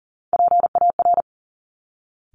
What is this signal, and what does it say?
Morse code, heard as a keyed tone, about 39 wpm, 700 Hz. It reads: PAR
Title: Top Three Letter Combinations in Words 40wpm